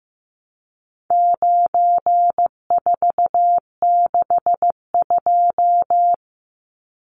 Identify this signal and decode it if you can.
Morse code, heard as a keyed tone, about 15 wpm, 700 Hz. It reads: 9462
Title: Numbers - 4-Digit 15wpm